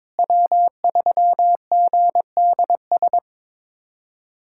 Morse code, heard as a keyed tone, about 22 wpm, 700 Hz. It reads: W3GDS
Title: USA Callsign DE Callsign 22wpm